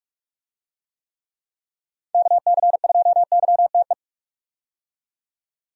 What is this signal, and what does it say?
Morse code, heard as a keyed tone, about 44 wpm, 700 Hz. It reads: KC2YTE
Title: USA Call Signs - 6 Characters - 45wpm